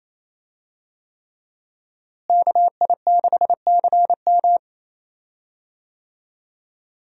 Morse code, heard as a keyed tone, about 28 wpm, 700 Hz. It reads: KI6CM
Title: USA Call Signs - 5 Characters - Repeated 3x - 28wpm